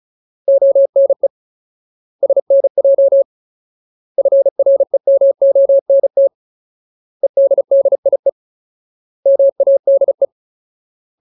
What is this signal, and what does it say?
Morse code, heard as a keyed tone, about 35 wpm, 550 Hz. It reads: ONE SNJ FREMONT EDDIE MADE
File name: Ultimate Word List / 35WPM Ultimate